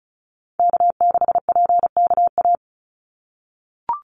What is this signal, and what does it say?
Morse code, heard as a keyed tone, about 36 wpm, 700 Hz. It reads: K6PKA
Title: USA Call Signs 35wpm